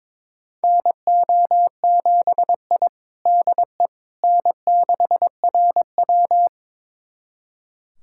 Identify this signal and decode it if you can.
Morse code, heard as a keyed tone, about 22 wpm, 700 Hz. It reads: NO7I DE N6RW
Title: USA Callsign DE Callsign - 4 Characters - 22wpm